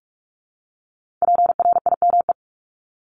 Morse code, heard as a keyed tone, about 46 wpm, 700 Hz. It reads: PRIME
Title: Multi-letter Phonemes 45wpm